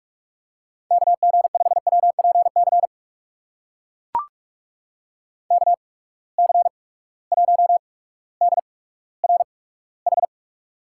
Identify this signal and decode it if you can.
Morse code, heard as a keyed tone, about 42 wpm, spaced as for 19 wpm, 700 Hz. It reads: KG5WPC KC1DRH
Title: USA Call Signs - 6 Characters - 8x Character Spacing - 45wpm